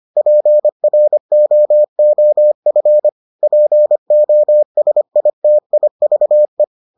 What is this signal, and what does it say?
Morse code, heard as a keyed tone, about 25 wpm, 600 Hz. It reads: PROOF POSITIVE